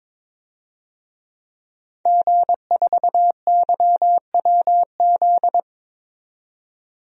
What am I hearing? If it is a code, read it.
Morse code, heard as a keyed tone, about 22 wpm, 700 Hz. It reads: G4YWZ
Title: DXCC-100 Call Signs - 5 Characters - 22wpm